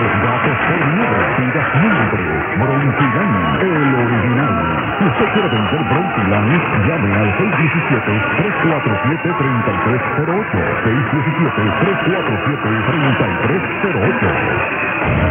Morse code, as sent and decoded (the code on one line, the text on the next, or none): none